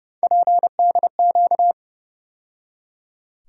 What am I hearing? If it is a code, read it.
Morse code, heard as a keyed tone, about 30 wpm, 700 Hz. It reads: PDQ
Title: USA Call Sign Postfixes - 3 Letters 30wpm